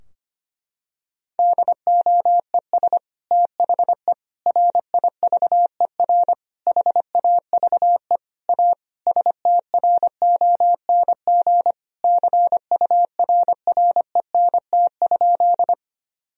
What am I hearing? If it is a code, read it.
Morse code, heard as a keyed tone, about 25 wpm, 700 Hz. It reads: DOES THE RIVER HAVE A STRONG CURRENT?
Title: Sentences from Top 1000 Words - Review Part 4 - 25wpm